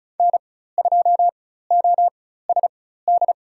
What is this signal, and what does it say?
Morse code, heard as a keyed tone, about 36 wpm, spaced as for 16 wpm, 700 Hz. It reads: N2OSD
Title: USA Call Signs - 5 Characters - 4x Character Spacing - 35wpm